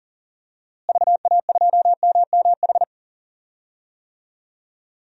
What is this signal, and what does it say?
Morse code, heard as a keyed tone, about 40 wpm, 700 Hz. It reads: VA2MMH
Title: DXCC-100 Call Signs - 6 Characters - Repeated 2x - 40wpm